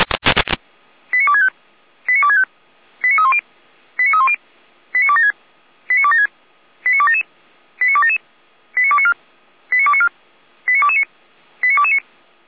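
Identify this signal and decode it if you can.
Five-tone selective call (ZVEI), heard as quick runs of five tones, tones of about 70 ms. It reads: 89265 89265 89310 89310 89276 89276 89277 89277 89294 89294 89229 89229
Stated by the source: ZVEI five-tone selective call received on SDR.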